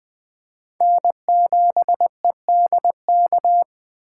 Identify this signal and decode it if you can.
Morse code, heard as a keyed tone, about 20 wpm, 700 Hz. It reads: N7EDK
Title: 2023 SOTA Chaser Call Signs - 20wpm